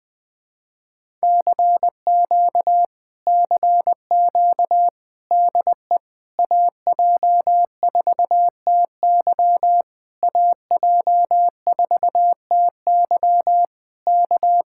Morse code, heard as a keyed tone, about 20 wpm, 700 Hz. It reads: CQ CQ DE AJ4TY AJ4TY K
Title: Speed-Racing Common QSO Phrases - 17wpm